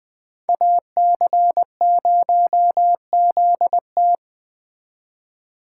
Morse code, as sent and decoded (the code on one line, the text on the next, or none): .- -.-. ----- --.. -
AC0ZT